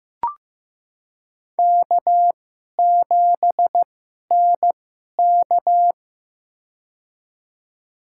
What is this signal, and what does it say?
Morse code, heard as a keyed tone, about 15 wpm, spaced as for 11 wpm, 700 Hz. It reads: K7NK
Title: USA Call Signs - 4 Characters - 2x Character Spacing - 15wpm